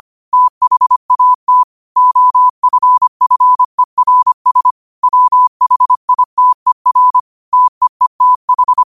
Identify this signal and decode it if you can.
Morse code, heard as a keyed tone, about 25 wpm, 1000 Hz. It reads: THAT OFFERS WHITER TEETH
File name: qotdfiles / 25wpm